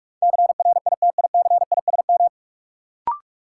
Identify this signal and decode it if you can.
Morse code, heard as a keyed tone, about 46 wpm, 700 Hz. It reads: CRITICISM
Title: Sentences from Top 1300 Words - New Word Drill - 45wpm